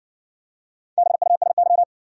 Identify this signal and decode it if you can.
Morse code, heard as a keyed tone, about 57 wpm, 700 Hz. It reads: BUSY